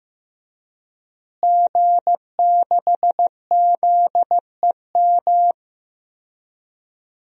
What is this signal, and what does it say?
Morse code, heard as a keyed tone, about 15 wpm, 700 Hz. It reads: G6ZEM
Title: DXCC-100 Call Signs - 5 Characters - Repeated 3x - 15wpm